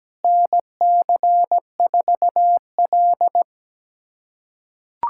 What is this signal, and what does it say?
Morse code, heard as a keyed tone, about 17 wpm, 700 Hz. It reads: NC4L
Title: USA Callsign DE Callsign - 4 Characters - 17wpm